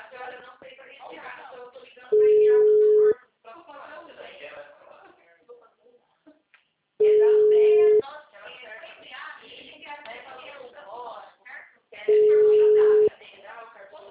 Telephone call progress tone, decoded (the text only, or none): ringback tone